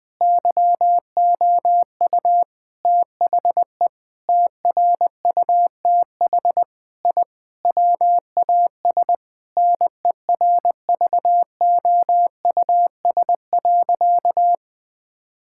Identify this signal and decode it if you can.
Morse code, heard as a keyed tone, about 20 wpm, 700 Hz. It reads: YOU THE TRUTH I WAS NERVOUS.
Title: Sentences with 5-Word Phrases - Spoken Beginning - 20wpm